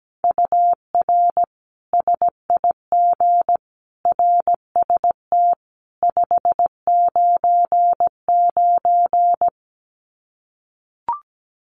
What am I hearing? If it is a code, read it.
Morse code, heard as a keyed tone, about 17 wpm, 700 Hz. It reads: UR SIG RST 599
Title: Common QSO Phrases 17wpm